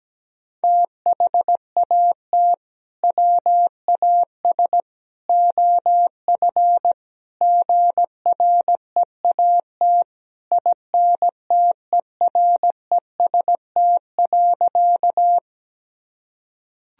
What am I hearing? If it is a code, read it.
Morse code, heard as a keyed tone, about 17 wpm, 700 Hz. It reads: THAT WAS OF GREAT INTEREST.
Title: Sentences from Top 1000 Words - Review Part 2 - 17wpm